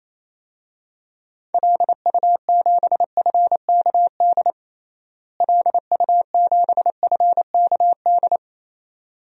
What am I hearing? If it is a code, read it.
Morse code, heard as a keyed tone, about 28 wpm, 700 Hz. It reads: LU7FKD LU7FKD
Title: DXCC-100 Call Signs - 6 Characters - Repeated 2x - 28wpm